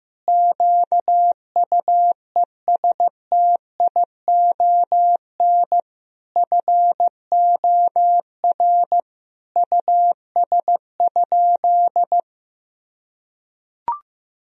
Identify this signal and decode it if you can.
Morse code, heard as a keyed tone, about 15 wpm, 700 Hz. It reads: QUESTION FOR US?
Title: Sentences from Top 1000 Words - Review Part 3 - 15wpm